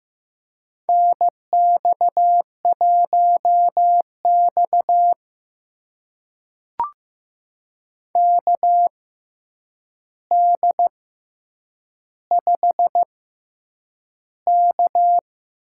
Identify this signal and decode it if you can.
Morse code, heard as a keyed tone, about 15 wpm, spaced as for 8 wpm, 700 Hz. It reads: NX1X KD5K
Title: USA Call Signs - 4 Characters - 6x Character Spacing - 15wpm